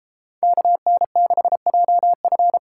Morse code, heard as a keyed tone, about 32 wpm, 700 Hz. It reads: KN6JF